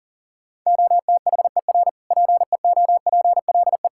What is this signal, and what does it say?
Morse code, heard as a keyed tone, about 40 wpm, 700 Hz. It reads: OTHER PEOPLE
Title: Sentences from Top 1000 Words - Review Part 5 - 40wpm